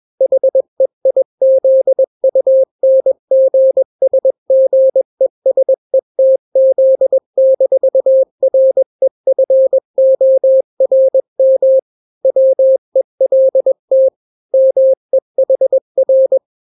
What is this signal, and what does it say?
Morse code, heard as a keyed tone, about 21 wpm, 530 Hz. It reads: HEIZUNGSGESETZ-REFORM WELT MEHR
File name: daily / 20wpm